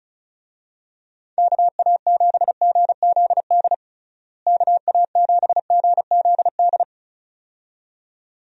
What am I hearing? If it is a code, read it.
Morse code, heard as a keyed tone, about 36 wpm, 700 Hz. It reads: KA7GZD KA7GZD
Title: USA Call Signs - 6 Characters - Repeated 2x - 35wpm